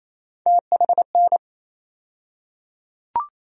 Morse code, heard as a keyed tone, about 27 wpm, 700 Hz. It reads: THN
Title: USA Call Sign Postfixes - 3 Letters 28wpm